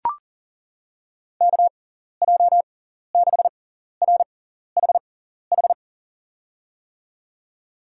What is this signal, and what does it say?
Morse code, heard as a keyed tone, about 38 wpm, spaced as for 14 wpm, 700 Hz. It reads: KJ6RHH
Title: USA Call Signs - 6 Characters - 6x Character Spacing - 40wpm